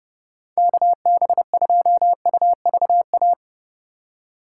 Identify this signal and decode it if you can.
Morse code, heard as a keyed tone, about 31 wpm, 700 Hz. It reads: KB2UVA
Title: USA Call Signs - 6 Characters - 3x Character Spacing - 30wpm